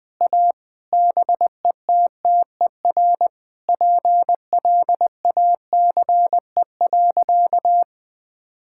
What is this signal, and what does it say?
Morse code, heard as a keyed tone, about 20 wpm, 700 Hz. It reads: A BETTER PLACE.
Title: Sentences with 7-Word Phrases - Spoken Beginning - 20wpm